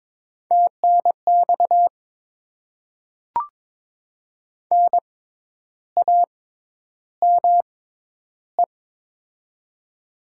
Morse code, heard as a keyed tone, about 22 wpm, spaced as for 10 wpm, 700 Hz. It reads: TNX NAME
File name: Top 20 QSO Elements - 6x Character Spacing - 22wpm